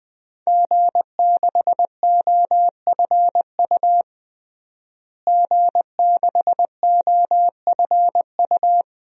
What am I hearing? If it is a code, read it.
Morse code, heard as a keyed tone, about 20 wpm, 700 Hz. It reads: G6OFU G6OFU
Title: DXCC-100 Call Signs - 5 Characters - Repeated 2x - 20wpm